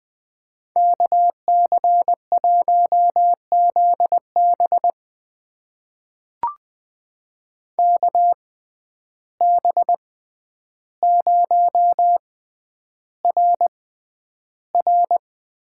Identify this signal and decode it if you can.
Morse code, heard as a keyed tone, about 20 wpm, spaced as for 10 wpm, 700 Hz. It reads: KC1ZB KB0RR
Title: USA Call Signs - 5 Characters - 6x Character Spacing - 20wpm